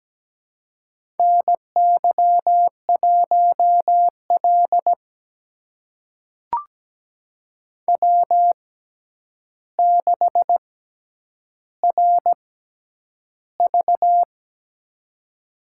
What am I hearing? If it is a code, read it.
Morse code, heard as a keyed tone, about 17 wpm, spaced as for 9 wpm, 700 Hz. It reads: NY1L W6RV
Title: USA Call Signs - 4 Characters - 6x Character Spacing - 17wpm